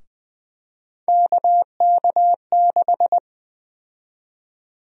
Morse code, heard as a keyed tone, about 20 wpm, 700 Hz. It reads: KK6
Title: USA Call Sign Prefixes - 20wpm